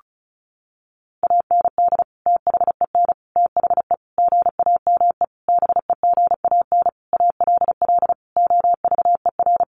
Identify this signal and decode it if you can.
Morse code, heard as a keyed tone, about 34 wpm, 700 Hz. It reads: AND THEN THE GAME BEGAN ALL OVER
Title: Sentences from Top 500 Words 35wpm